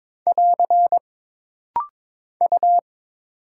Morse code, heard as a keyed tone, about 22 wpm, 700 Hz. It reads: <AR> U
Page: This rapid-fire timing is geared to push your ICR (Instant Character Recognition) skills.